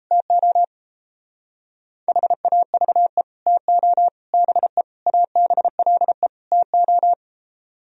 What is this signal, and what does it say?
Morse code, heard as a keyed tone, about 32 wpm, 700 Hz. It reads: TO HAVE TO BE ABLE TO
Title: Speed-Racing - Sets of 5 Words 28wpm